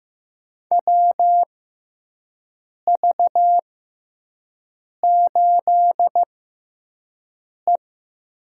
Morse code, heard as a keyed tone, about 15 wpm, spaced as for 5 wpm, 700 Hz. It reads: WV8E
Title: USA Call Signs - 6 Characters - 6x Character Spacing - 15wpm